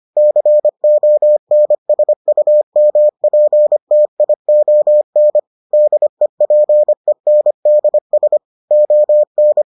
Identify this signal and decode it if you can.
Morse code, heard as a keyed tone, about 25 wpm, 600 Hz. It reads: CONSUMPTION DEPENDS ON